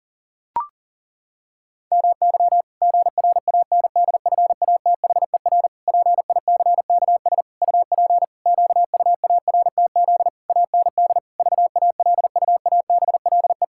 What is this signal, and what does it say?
Morse code, heard as a keyed tone, about 40 wpm, 700 Hz. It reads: MY GRANDFATHER PICKS UP QUARTZ AND VALUABLE
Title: Pangram Challenge v2 - 40wpm